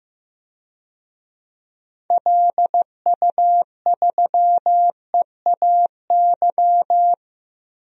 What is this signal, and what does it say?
Morse code, heard as a keyed tone, about 15 wpm, 700 Hz. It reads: LU3EAY
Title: DXCC-100 Call Signs - 6 Characters - Repeated 3x - 15wpm